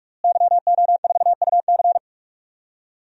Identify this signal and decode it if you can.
Morse code, heard as a keyed tone, about 46 wpm, 700 Hz. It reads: YO4UC
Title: DXCC-100 Call Signs - 5 Characters - Repeated 2x - 45wpm